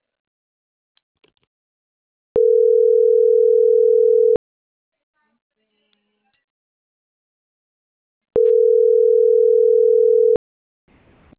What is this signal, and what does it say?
Ringback tone (North America), 440 + 480 Hz.